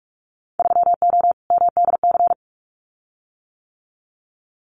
Morse code, heard as a keyed tone, about 46 wpm, 700 Hz. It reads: 3O MDC
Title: Winter Field Day Exchanges Encore 45wpm